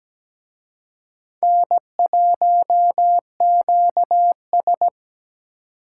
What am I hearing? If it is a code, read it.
Morse code, heard as a keyed tone, about 20 wpm, 700 Hz. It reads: N1QS
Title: Speed Racing - USA 1x2 Call Signs 17wpm